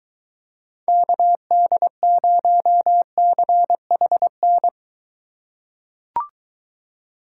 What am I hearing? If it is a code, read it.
Morse code, heard as a keyed tone, about 23 wpm, 700 Hz. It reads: KD0CHN